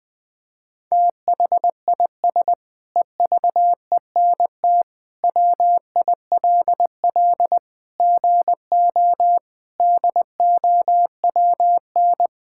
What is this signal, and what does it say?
Morse code, heard as a keyed tone, about 21 wpm, 700 Hz. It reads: THIS EVENT WILL GO DOWN
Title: Sentences with 7-Word Phrases - 20wpm